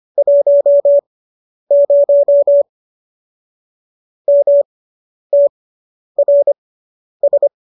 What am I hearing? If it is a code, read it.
Morse code, heard as a keyed tone, about 25 wpm, spaced as for 10 wpm, 570 Hz. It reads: 10 MTRS